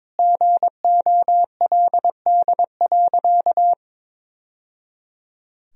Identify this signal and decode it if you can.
Morse code, heard as a keyed tone, about 22 wpm, 700 Hz. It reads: GOLD.
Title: Sentences from Top 1000 Words - Review Part 3 - 22wpm